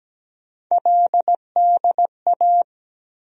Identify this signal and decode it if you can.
Morse code, heard as a keyed tone, about 17 wpm, 700 Hz. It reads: LDA
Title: USA Call Sign Postfixes - 3 Letters 17wpm